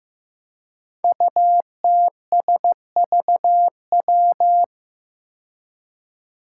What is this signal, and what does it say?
Morse code, heard as a keyed tone, about 15 wpm, 700 Hz. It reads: UTSVW
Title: Five Challenging Letter-Number Combinations 15wpm